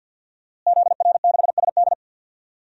Morse code, heard as a keyed tone, about 52 wpm, 700 Hz. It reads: ZR6SD
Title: DXCC-100 Call Signs - 5 Characters - Repeated 3x - 50wpm